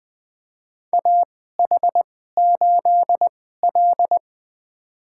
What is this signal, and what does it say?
Morse code, heard as a keyed tone, about 20 wpm, spaced as for 14 wpm, 700 Hz. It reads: AH8L
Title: USA Call Signs - 4 Characters - 2x Character Spacing - 20wpm